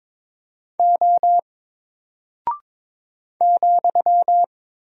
Morse code, heard as a keyed tone, about 22 wpm, 700 Hz. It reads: O ,
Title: Single Character (Letter:Number:Punctuation:Prosign) 22wpm